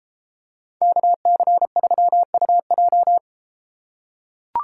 Morse code, heard as a keyed tone, about 33 wpm, 700 Hz. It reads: KC3UJ